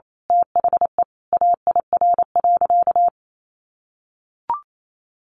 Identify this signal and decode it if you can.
Morse code, heard as a keyed tone, about 28 wpm, 700 Hz. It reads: THE AIR.
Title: A Sound Of Thunder 28wpm